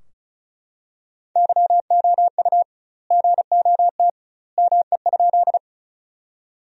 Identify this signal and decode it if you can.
Morse code, heard as a keyed tone, about 34 wpm, 700 Hz. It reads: YOU GOT ME?
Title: 2x Word-Spacing - Sentences from Top 300 Words - 35wpm